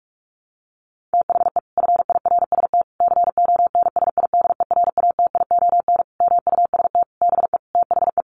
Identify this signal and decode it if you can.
Morse code, heard as a keyed tone, about 44 wpm, 700 Hz. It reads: THE FIRST CONSIDERATION MUST BE THE
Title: Sentences from Top 1500 Words 45wpm